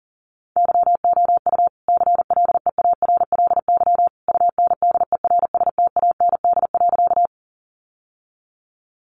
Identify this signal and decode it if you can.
Morse code, heard as a keyed tone, about 40 wpm, 700 Hz. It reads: YOU CLEARLY UNDERSTAND.
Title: Sentences from Top 1300 Words 40wpm